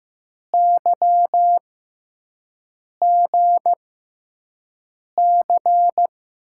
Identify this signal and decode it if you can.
Morse code, heard as a keyed tone, about 15 wpm, spaced as for 5 wpm, 700 Hz. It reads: YGC
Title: USA Call Signs - 6 Characters - 6x Character Spacing - 15wpm